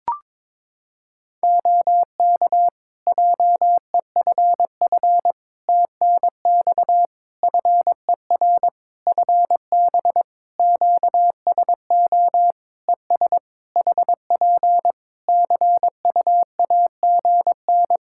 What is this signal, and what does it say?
Morse code, heard as a keyed tone, about 22 wpm, 700 Hz. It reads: OK JEFF TNX FER FB QSO ES HP CUAGN
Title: Practice QSO - Exchange 4 - Ending - 22wpm